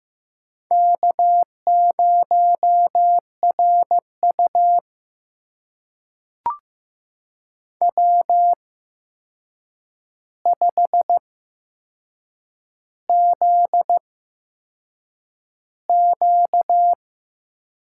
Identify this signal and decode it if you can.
Morse code, heard as a keyed tone, about 15 wpm, spaced as for 6 wpm, 700 Hz. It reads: K0RU W5ZQ
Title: USA Call Signs - 4 Characters - 8x Character Spacing - 15wpm